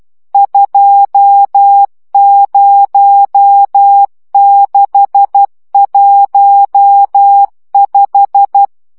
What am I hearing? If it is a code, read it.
Morse code, heard as a keyed tone, about 12 wpm, 800 Hz. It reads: 20615
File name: morze vizsga 2011-3 10